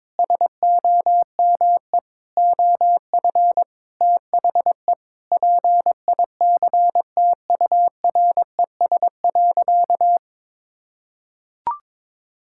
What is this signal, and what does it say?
Morse code, heard as a keyed tone, about 22 wpm, 700 Hz. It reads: SOME OF THE PICTURES.
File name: practice / 22wpm / Sentences with 8-Word Phrases - 22wpm